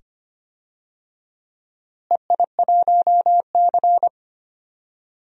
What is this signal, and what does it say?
Morse code, heard as a keyed tone, about 25 wpm, 700 Hz. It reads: EI1C
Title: DXCC-100 Call Signs - 4 Characters - Repeated 2x - 25wpm